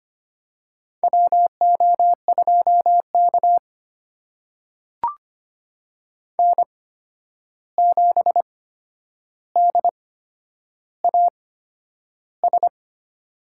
Morse code, heard as a keyed tone, about 25 wpm, spaced as for 10 wpm, 700 Hz. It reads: WO2K N7DAS
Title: USA Call Signs - 8x Character Spacing - 25wpm